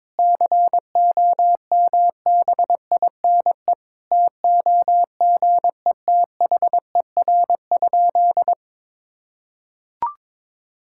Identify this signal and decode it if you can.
Morse code, heard as a keyed tone, about 22 wpm, 700 Hz. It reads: COMBINE TOGETHER?
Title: Sentences from Top 1600 Words 22wpm